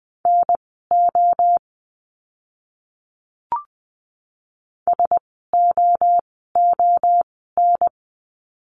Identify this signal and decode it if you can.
Morse code, heard as a keyed tone, about 20 wpm, spaced as for 15 wpm, 700 Hz. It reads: NO SOON
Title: 054 Lesson 10f - Letter r - Words 20wpm